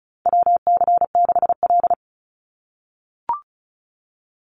Morse code, heard as a keyed tone, about 34 wpm, 700 Hz. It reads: WC6L
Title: USA Call Signs 35wpm